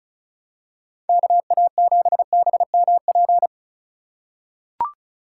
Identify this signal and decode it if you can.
Morse code, heard as a keyed tone, about 36 wpm, 700 Hz. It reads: KA7BMP
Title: USA Call Signs - 6 Characters - 4x Character Spacing - 35wpm